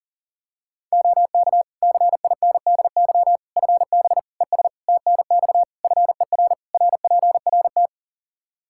Morse code, heard as a keyed tone, about 40 wpm, 700 Hz. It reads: OK CINDY FB ES TNX FER RPRT
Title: Practice QSO - Exchange 2 - Info - 40wpm